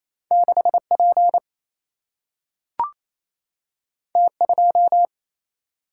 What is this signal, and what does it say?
Morse code, heard as a keyed tone, about 28 wpm, 700 Hz. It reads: 6P T2
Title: Single-Letter Single-Number 28wpm